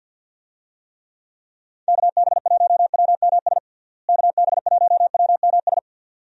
Morse code, heard as a keyed tone, about 51 wpm, 700 Hz. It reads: KB1WMS KB1WMS
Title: USA Call Signs - 6 Characters - Repeated 2x - 50wpm